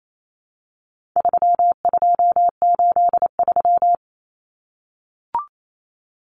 Morse code, heard as a keyed tone, about 28 wpm, 700 Hz. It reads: 3283
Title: Common Cut Numbers - 4-Digit 28wpm